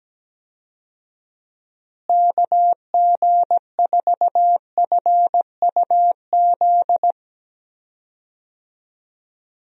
Morse code, heard as a keyed tone, about 17 wpm, 700 Hz. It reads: KG4FUZ
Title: USA Call Signs - 6 Characters - Repeated 2x - 17wpm